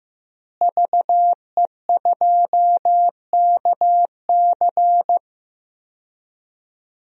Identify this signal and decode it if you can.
Morse code, heard as a keyed tone, about 15 wpm, 700 Hz. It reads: VE2KC
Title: DXCC-100 Call Signs - 5 Characters - Repeated 3x - 15wpm